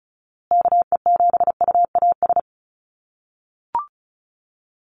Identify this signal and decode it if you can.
Morse code, heard as a keyed tone, about 34 wpm, 700 Hz. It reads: KE7UAS
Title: USA Call Signs Repeated 3x 35wpm